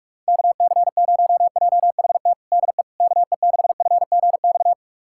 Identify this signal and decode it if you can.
Morse code, heard as a keyed tone, about 46 wpm, 700 Hz. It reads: KC0JHT DE KE6FGX
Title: USA Callsign DE Callsign - 6 Characters - 45wpm